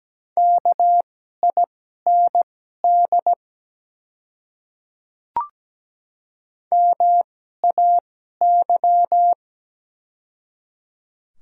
Morse code, heard as a keyed tone, about 17 wpm, spaced as for 12 wpm, 700 Hz. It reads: KIND MAY
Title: 2x Character Spacing - Top 200 Words - 17wpm